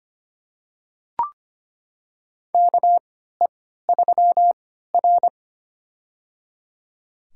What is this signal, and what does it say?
Morse code, heard as a keyed tone, about 25 wpm, spaced as for 14 wpm, 700 Hz. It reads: KE3R
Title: USA Call Signs - 4 Characters - 3x Character Spacing - 25wpm